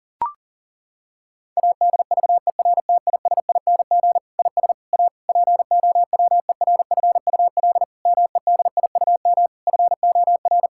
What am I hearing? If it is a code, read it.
Morse code, heard as a keyed tone, about 40 wpm, 700 Hz. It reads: ADVERTISING IS A POWERFUL MEDIUM FOR
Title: Sentences from Top 1000 Words - Review Part 6 - 40wpm